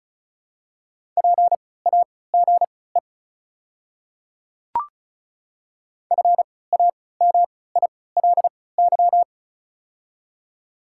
Morse code, heard as a keyed tone, about 35 wpm, spaced as for 20 wpm, 700 Hz. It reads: PAGE FAMILY
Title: 3x Character Spacing - Top 500 Words - 35wpm